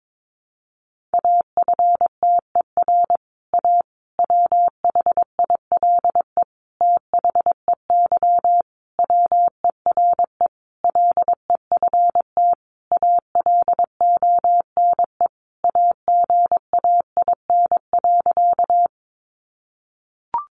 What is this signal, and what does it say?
Morse code, heard as a keyed tone, about 22 wpm, 700 Hz. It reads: AFTER A WHILE THEY WERE LEFT ALONE AGAIN.
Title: Sentences from Top 1000 Words 22wpm